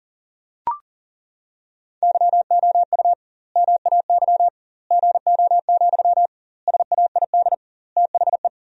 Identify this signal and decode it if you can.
Morse code, heard as a keyed tone, about 40 wpm, 700 Hz. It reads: YOU MAY GO, SAID THE
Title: 2x Word-Spacing - Sentences from Top 400 Words - 40wpm